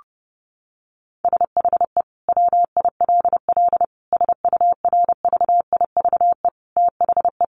Morse code, heard as a keyed tone, about 30 wpm, 700 Hz. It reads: SHE WILL SURVIVE THE
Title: Sentences from Top 1300 Words 30wpm